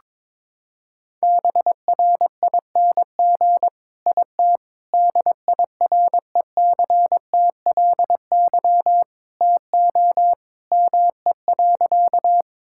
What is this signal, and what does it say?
Morse code, heard as a keyed tone, about 22 wpm, 700 Hz. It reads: BRING IT DIRECTLY TO ME.
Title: Sentences from Top 1000 Words - Review Part 5 - 22wpm